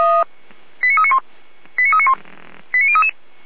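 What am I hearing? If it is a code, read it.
Five-tone selective call (ZVEI), heard as quick runs of five tones, tones of about 70 ms. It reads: 89391 89391 89033
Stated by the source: ZVEI five-tone selective call 89187 received by SDR.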